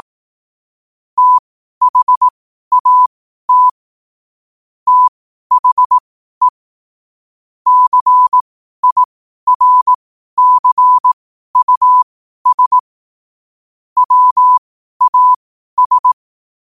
Morse code, heard as a keyed tone, about 18 wpm, spaced as for 12 wpm, 1000 Hz. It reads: THAT THE CIRCUS WAS